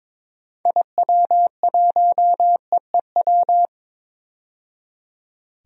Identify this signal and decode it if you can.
Morse code, heard as a keyed tone, about 22 wpm, 700 Hz. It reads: IW1EEW
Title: DXCC-100 Call Signs - 6 Characters - 22wpm